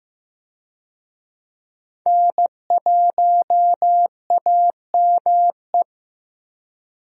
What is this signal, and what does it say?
Morse code, heard as a keyed tone, about 15 wpm, 700 Hz. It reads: N1AME
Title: USA Call Signs - 5 Characters - Repeated 2x - 15wpm